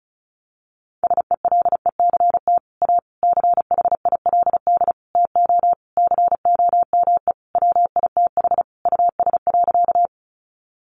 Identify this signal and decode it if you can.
Morse code, heard as a keyed tone, about 34 wpm, 700 Hz. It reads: SELECT A CHILD TO COME WITH US.
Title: Sentences from Top 1300 Words 35wpm